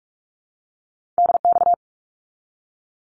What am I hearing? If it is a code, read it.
Morse code, heard as a keyed tone, about 44 wpm, 700 Hz. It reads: DX
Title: Abbreviations 45wpm